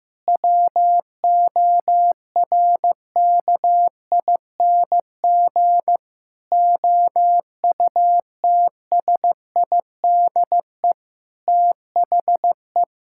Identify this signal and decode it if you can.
Morse code, heard as a keyed tone, about 15 wpm, 700 Hz. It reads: WORKING OUTSIDE THE
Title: Sentences from Top 1000 Words - Review Part 6 - 15wpm